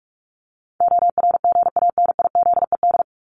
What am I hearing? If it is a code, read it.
Morse code, heard as a keyed tone, about 44 wpm, 700 Hz. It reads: ORGANIZED
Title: Top 1001-1250 Words 45wpm